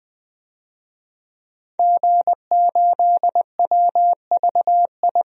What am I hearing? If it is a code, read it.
Morse code, heard as a keyed tone, about 20 wpm, 700 Hz. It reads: G8WVI
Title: DXCC-100 Call Signs - 5 Characters - 20wpm